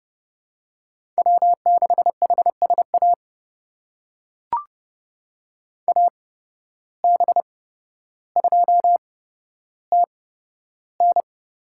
Morse code, heard as a keyed tone, about 30 wpm, spaced as for 13 wpm, 700 Hz. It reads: W6HSA AB2TN
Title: USA Call Signs - 8x Character Spacing - 30wpm